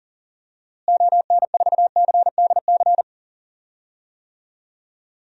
Morse code, heard as a keyed tone, about 40 wpm, 700 Hz. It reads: ON4CDC
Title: DXCC-100 Call Signs - 6 Characters - Repeated 3x - 40wpm